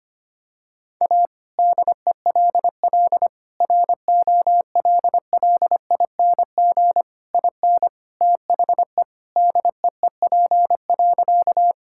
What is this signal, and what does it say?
Morse code, heard as a keyed tone, about 25 wpm, 700 Hz. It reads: A DELL ROLLING IN THE DEEP.
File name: practice / 25wpm / All-Age Appropriate Jokes - 25wpm